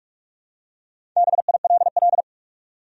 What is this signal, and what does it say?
Morse code, heard as a keyed tone, about 45 wpm, 700 Hz. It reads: BILL
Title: USA Common Names - Max 4 Letters - 45wpm